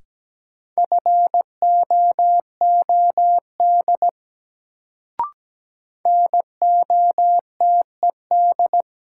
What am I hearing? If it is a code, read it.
Morse code, heard as a keyed tone, about 17 wpm, 700 Hz. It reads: FOOD NOTED
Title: Sets of 5 Words - New Word Drill - 17wpm